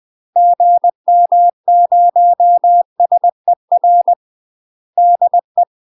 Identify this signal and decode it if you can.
Morse code, heard as a keyed tone, about 20 wpm, 700 Hz. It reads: GM0SER DE
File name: UK MASTER QSOs TRAINING / 20WPM / UK MASTER QSOs 0043